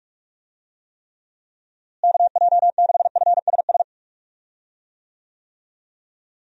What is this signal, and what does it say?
Morse code, heard as a keyed tone, about 46 wpm, 700 Hz. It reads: KJ6FSS
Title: USA Call Signs - 6 Characters - Repeated 2x - 45wpm